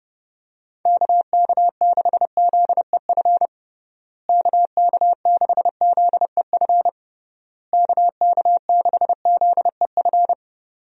Morse code, heard as a keyed tone, about 31 wpm, 700 Hz. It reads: KK6ZEF KK6ZEF KK6ZEF
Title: USA Call Signs - 6 Characters - Repeated 3x - 30wpm